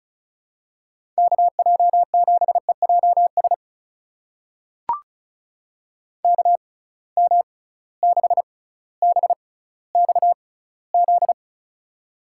Morse code, heard as a keyed tone, about 34 wpm, spaced as for 18 wpm, 700 Hz. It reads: KJ7EJS KM6BXZ
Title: USA Call Signs - 6 Characters - 6x Character Spacing - 35wpm